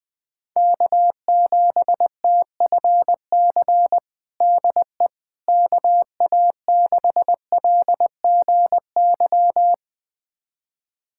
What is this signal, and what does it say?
Morse code, heard as a keyed tone, about 20 wpm, 700 Hz. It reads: K7TFC DE KA6LGY
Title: USA Callsign DE Callsign 20wpm